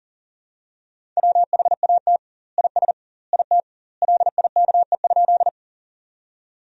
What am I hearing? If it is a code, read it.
Morse code, heard as a keyed tone, about 37 wpm, 700 Hz. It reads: WHAT IS IT LIKE?
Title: 2x Word-Spacing - Sentences from Top 100 Words - 40wpm